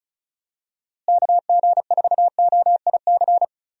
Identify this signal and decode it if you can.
Morse code, heard as a keyed tone, about 36 wpm, 700 Hz. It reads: KG4OIC
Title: USA Call Signs - 6 Characters - Repeated 2x - 35wpm